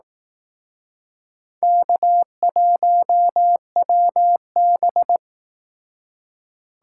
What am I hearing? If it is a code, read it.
Morse code, heard as a keyed tone, about 19 wpm, 700 Hz. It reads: K1WB